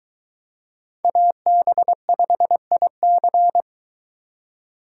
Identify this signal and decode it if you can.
Morse code, heard as a keyed tone, about 22 wpm, 700 Hz. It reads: AB5IC